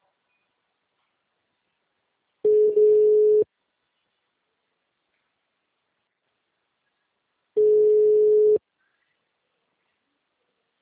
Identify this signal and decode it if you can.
Ringback tone (Europe, 425 Hz).
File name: out- -2002- - - .1197